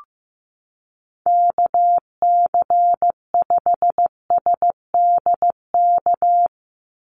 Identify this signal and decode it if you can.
Morse code, heard as a keyed tone, about 15 wpm, 700 Hz. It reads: KC5SDK
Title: USA Call Signs 15wpm